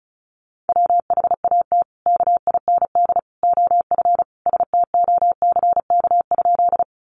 Morse code, heard as a keyed tone, about 34 wpm, 700 Hz. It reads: WHAT KIND OF STOCK?
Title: Sentences from Top 800 Words 35wpm